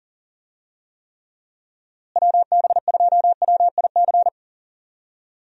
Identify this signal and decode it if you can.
Morse code, heard as a keyed tone, about 40 wpm, 700 Hz. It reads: WB2WIC
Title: USA Call Signs - 6 Characters - Repeated 3x - 40wpm